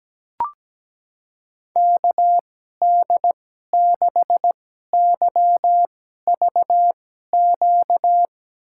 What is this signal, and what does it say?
Morse code, heard as a keyed tone, about 17 wpm, spaced as for 12 wpm, 700 Hz. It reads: KD6YVQ
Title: USA Call Signs - 6 Characters - 2x Character Spacing - 17wpm